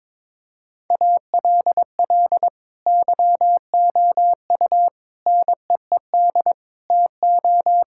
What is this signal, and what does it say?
Morse code, heard as a keyed tone, about 21 wpm, 700 Hz. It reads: ALL YOU NEED TO
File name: practice / 22wpm / Sets of 6 Words - 22wpm